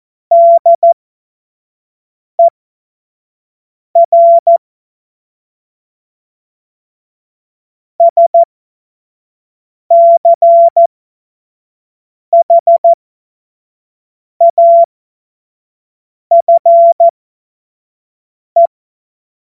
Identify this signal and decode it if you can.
Morse code, heard as a keyed tone, about 14 wpm, spaced as for 5 wpm, 680 Hz. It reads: DER SCHAFE
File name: Morsekurs / Witze / Witz 18. April 2023 25 F70